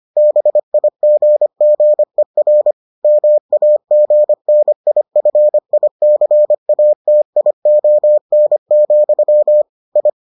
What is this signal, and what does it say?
Morse code, heard as a keyed tone, about 25 wpm, 600 Hz. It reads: BIGGER MAGNIFICATION, I